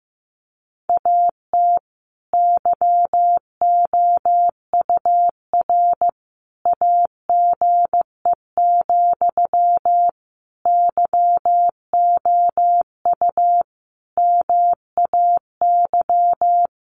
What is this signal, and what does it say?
Morse code, heard as a keyed tone, about 15 wpm, 700 Hz. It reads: AT YOUR AGE, YOU MAY
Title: Sentences from Top 1500 Words 15wpm